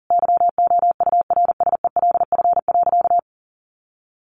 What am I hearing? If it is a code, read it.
Morse code, heard as a keyed tone, about 40 wpm, 700 Hz. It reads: YOURSELF.
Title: Sentences from Top 1200 Words 40wpm